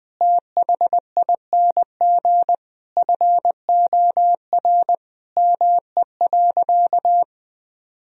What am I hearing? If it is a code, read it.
Morse code, heard as a keyed tone, about 20 wpm, 700 Hz. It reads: THING FOR ME.
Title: Sentences from Top 1000 Words - Review Part 4 - 20wpm